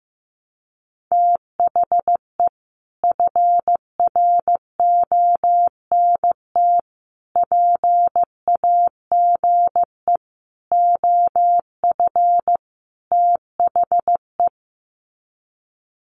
Morse code, heard as a keyed tone, about 15 wpm, 700 Hz. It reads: THE FRONT PAGE OF THE
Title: Sets of 5 Words Encore 15wpm